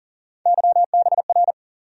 Morse code, heard as a keyed tone, about 40 wpm, 700 Hz. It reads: YBR
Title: USA Call Sign Postfixes - 3 Letters 40wpm